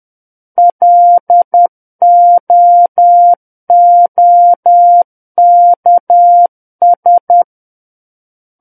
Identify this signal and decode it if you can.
Morse code, heard as a keyed tone, about 10 wpm, 700 Hz. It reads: LOOKS
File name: Book One / 01-Chapter 1 - 10 WPM